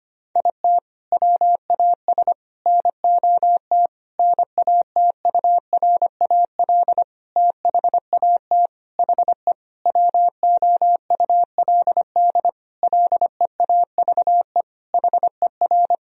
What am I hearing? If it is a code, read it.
Morse code, heard as a keyed tone, about 25 wpm, 700 Hz. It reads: IT WAS NOT NATURAL THAT HE WOULD LEAVE HER
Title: Sentences from Top 1000 Words - Review Part 4 - 25wpm